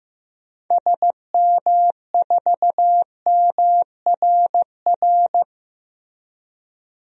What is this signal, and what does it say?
Morse code, heard as a keyed tone, about 15 wpm, 700 Hz. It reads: SM4MRR
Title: DXCC-100 Call Signs - 6 Characters - Repeated 2x - 15wpm